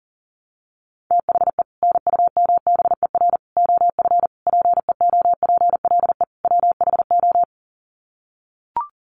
Morse code, heard as a keyed tone, about 40 wpm, 700 Hz. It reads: THE NUMBER OF PEOPLE WHO
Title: Sets of 5 Words Encore 40wpm